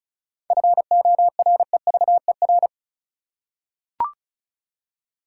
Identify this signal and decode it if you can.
Morse code, heard as a keyed tone, about 36 wpm, 700 Hz. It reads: FOREVER
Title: Seven Letter Words - 35wpm